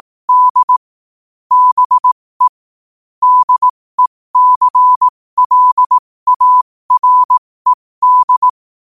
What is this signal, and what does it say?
Morse code, heard as a keyed tone, about 18 wpm, spaced as for 15 wpm, 1000 Hz. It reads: D BE DECLARED
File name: qotdfiles / 15wpm